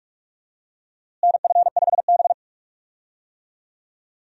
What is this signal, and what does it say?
Morse code, heard as a keyed tone, about 46 wpm, 700 Hz. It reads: NF5B
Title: USA Call Signs - 4 Characters - Repeated 2x - 45wpm